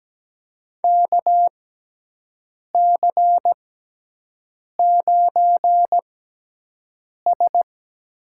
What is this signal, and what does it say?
Morse code, heard as a keyed tone, about 17 wpm, spaced as for 6 wpm, 700 Hz. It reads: KC9S
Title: USA Call Signs - 6 Characters - 6x Character Spacing - 17wpm